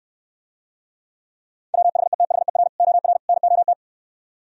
Morse code, heard as a keyed tone, about 68 wpm, 700 Hz. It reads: WHETHER OR NOT